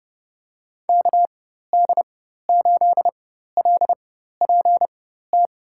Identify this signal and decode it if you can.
Morse code, heard as a keyed tone, about 30 wpm, spaced as for 14 wpm, 700 Hz. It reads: KD8LPT
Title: USA Call Signs - 4x Character Spacing - 30wpm